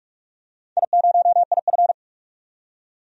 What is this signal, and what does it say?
Morse code, heard as a keyed tone, about 45 wpm, 700 Hz. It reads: I0IF
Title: DXCC-100 Call Signs - 4 Characters - 45wpm